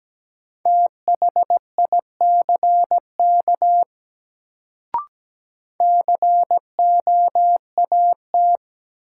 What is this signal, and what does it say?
Morse code, heard as a keyed tone, about 17 wpm, 700 Hz. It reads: THICK COAT
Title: Sentences from Top 1600 Words - New Word Drill - 17wpm